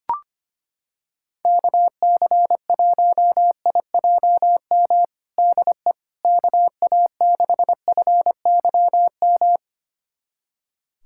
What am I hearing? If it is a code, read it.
Morse code, heard as a keyed tone, about 25 wpm, 700 Hz. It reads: KC1IJM DE KA6FYM
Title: USA Callsign DE Callsign - 6 Characters - 25wpm